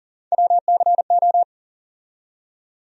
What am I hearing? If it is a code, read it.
Morse code, heard as a keyed tone, about 40 wpm, 700 Hz. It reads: WCO
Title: USA Call Sign Postfixes - 3 Letters 40wpm